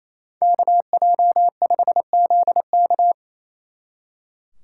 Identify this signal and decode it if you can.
Morse code, heard as a keyed tone, about 34 wpm, 700 Hz. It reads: KJ5ZK
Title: Speed Racing - USA 2x2 Call Signs 28wpm